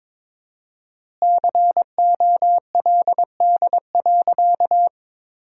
Morse code, heard as a keyed tone, about 21 wpm, 700 Hz. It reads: COLD.
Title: 4x Word-Spacing - Sentences from Top 400 Words - 22wpm